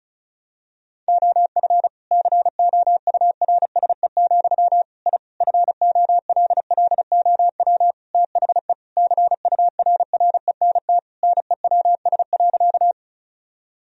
Morse code, heard as a keyed tone, about 34 wpm, 700 Hz. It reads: OF COURSE, I FOLLOW THE CURRENT NEWS.
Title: Sentences from Top 1000 Words - Review Part 4 - 35wpm